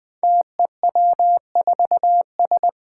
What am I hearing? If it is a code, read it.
Morse code, heard as a keyed tone, about 20 wpm, 700 Hz. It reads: TEW4S
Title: Five Challenging Letter-Number Combinations 20wpm